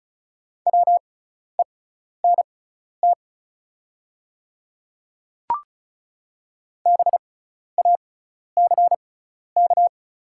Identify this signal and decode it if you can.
Morse code, heard as a keyed tone, about 36 wpm, spaced as for 12 wpm, 690 Hz. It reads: WENT BACK
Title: 6x Character Spacing - Top 300 Words - 35wpm